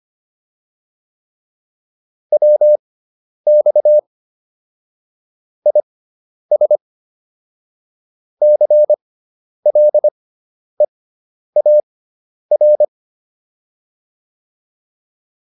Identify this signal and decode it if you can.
Morse code, heard as a keyed tone, about 25 wpm, spaced as for 10 wpm, 600 Hz. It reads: WX IS CLEAR